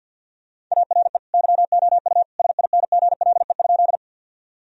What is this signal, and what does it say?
Morse code, heard as a keyed tone, about 50 wpm, 700 Hz. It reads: ARE YOU SINGLE?
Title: Sentences from Top 1000 Words - Review Part 4 - 50wpm